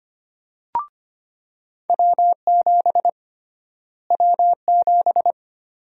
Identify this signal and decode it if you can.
Morse code, heard as a keyed tone, about 25 wpm, 700 Hz. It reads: W7 W7
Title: USA Call Sign Prefixes - Repeated 2x - 25wpm